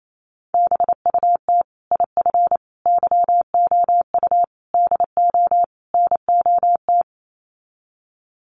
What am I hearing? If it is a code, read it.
Morse code, heard as a keyed tone, about 28 wpm, 690 Hz. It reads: BUT IF YOU DO NOT
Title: Sets of 5 Words Encore 28wpm